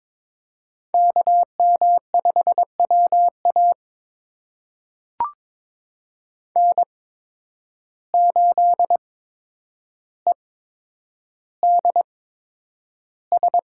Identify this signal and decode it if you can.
Morse code, heard as a keyed tone, about 22 wpm, spaced as for 9 wpm, 700 Hz. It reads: KM5WA N8EDS
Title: USA Call Signs - 5 Characters - 8x Character Spacing - 22wpm